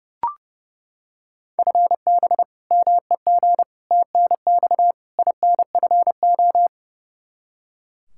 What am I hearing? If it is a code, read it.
Morse code, heard as a keyed tone, about 28 wpm, 700 Hz. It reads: FB MEG TNX INFO
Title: Practice QSO - Exchange 3 - More Info - 30wpm